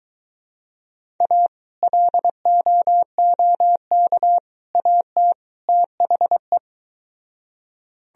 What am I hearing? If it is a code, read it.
Morse code, heard as a keyed tone, about 22 wpm, 700 Hz. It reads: A LOOK AT THE